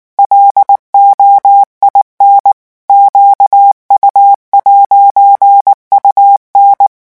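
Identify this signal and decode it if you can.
Morse code, heard as a keyed tone, about 19 wpm, 800 Hz. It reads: LOIN QU'UN
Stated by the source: French Morse revision 8 drill of random five-letter groups at 18 words per minute.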